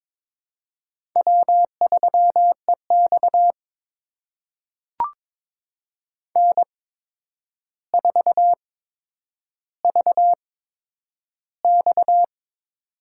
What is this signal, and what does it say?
Morse code, heard as a keyed tone, about 22 wpm, spaced as for 9 wpm, 700 Hz. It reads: W3EX N4VX
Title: USA Call Signs - 4 Characters - 8x Character Spacing - 22wpm